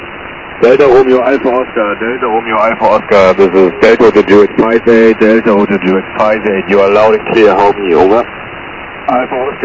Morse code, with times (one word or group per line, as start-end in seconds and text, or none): none